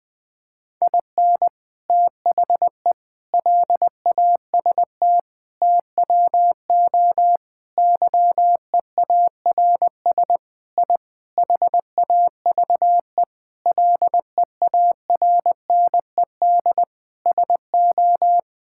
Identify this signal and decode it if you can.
Morse code, heard as a keyed tone, about 20 wpm, 700 Hz. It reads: IN THE LAST TWO YEARS I HAVE LEARNED SO
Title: Sentences with 5-Word Phrases - 20wpm